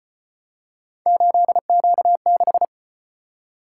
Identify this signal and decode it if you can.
Morse code, heard as a keyed tone, about 35 wpm, 700 Hz. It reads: 8Q6